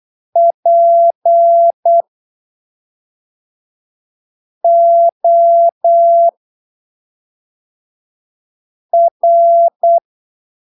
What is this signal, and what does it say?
Morse code, heard as a keyed tone, about 8 wpm, spaced as for 3 wpm, 680 Hz. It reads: POR
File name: arquivos / texto / texto 11 08ppm